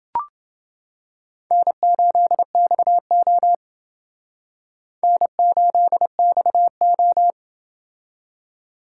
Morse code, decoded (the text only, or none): N8XO N8XO